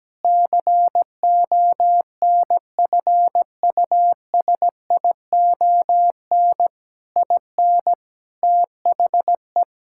Morse code, heard as a keyed tone, about 17 wpm, 700 Hz. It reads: CONFUSION IN THE
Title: Sentences from Top 1600 Words 17wpm